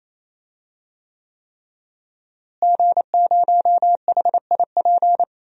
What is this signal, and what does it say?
Morse code, heard as a keyed tone, about 28 wpm, 700 Hz. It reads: G0HIP
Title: DXCC-100 Call Signs - 5 Characters - Repeated 3x - 28wpm